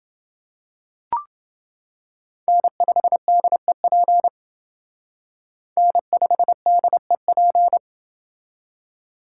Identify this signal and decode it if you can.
Morse code, heard as a keyed tone, about 28 wpm, 700 Hz. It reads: N5DEP N5DEP